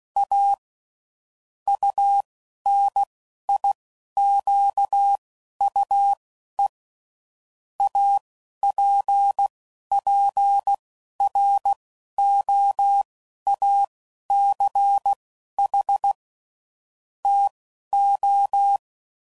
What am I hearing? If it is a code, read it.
Morse code, heard as a keyed tone, about 16 wpm, spaced as for 11 wpm, 800 Hz. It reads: A UNIQUE APPROACH TO
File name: CW Code Groups @10WPM / Technical Subjects / Gates Intro